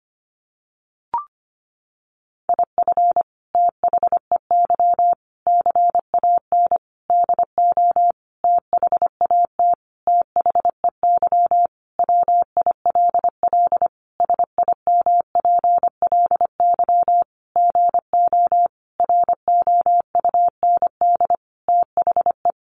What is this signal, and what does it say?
Morse code, heard as a keyed tone, about 25 wpm, 700 Hz. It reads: IF THEY CAN DO THAT THEY WILL SIMPLY GO ROUND THE
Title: Sentences from Top 600 Words 25wpm